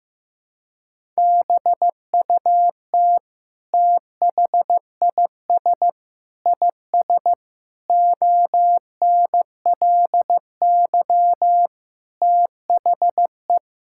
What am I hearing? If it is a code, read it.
Morse code, heard as a keyed tone, about 15 wpm, 700 Hz. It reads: BUT THIS IS ONLY THE
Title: Sentences from Top 1000 Words - Review Part 5 - 15wpm